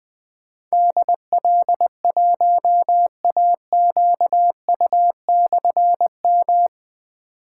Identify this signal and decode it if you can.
Morse code, heard as a keyed tone, about 20 wpm, 700 Hz. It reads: DL1AQU/M
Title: Difficult Call Signs 20wpm